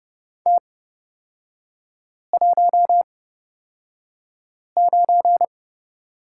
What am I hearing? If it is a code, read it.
Morse code, heard as a keyed tone, about 30 wpm, spaced as for 5 wpm, 700 Hz. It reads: T19
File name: course / 30:10wpm / 236 Lesson 38e - Letter q - Three Characters 30wpm